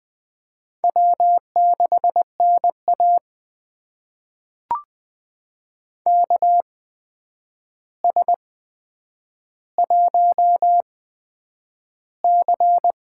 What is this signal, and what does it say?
Morse code, heard as a keyed tone, about 20 wpm, spaced as for 9 wpm, 700 Hz. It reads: W6NA KS1C
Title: USA Call Signs - 4 Characters - 8x Character Spacing - 20wpm